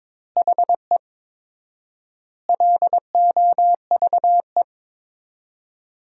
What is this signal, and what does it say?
Morse code, heard as a keyed tone, about 22 wpm, 700 Hz. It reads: HE LOVE
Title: 4x Word-Spacing - Sentences from Top 400 Words - 22wpm